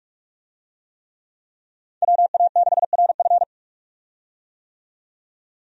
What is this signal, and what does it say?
Morse code, heard as a keyed tone, about 46 wpm, 700 Hz. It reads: WA6RF
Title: USA Call Signs - 5 Characters - 45wpm